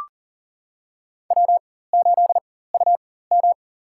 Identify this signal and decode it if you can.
Morse code, heard as a keyed tone, about 40 wpm, spaced as for 19 wpm, 690 Hz. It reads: W8UM
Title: USA Call Signs - 4x Character Spacing - 40wpm